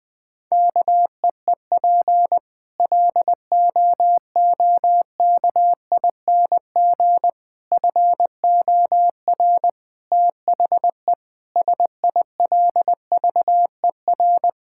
Morse code, heard as a keyed tone, about 20 wpm, 700 Hz. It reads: KEEP LOOKING FOR THE SILVER
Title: Difficult Sentences with Hints - 20wpm